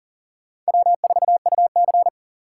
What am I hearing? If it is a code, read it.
Morse code, heard as a keyed tone, about 40 wpm, 700 Hz. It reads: W4UC
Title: USA Call Signs - 4x Character Spacing - 40wpm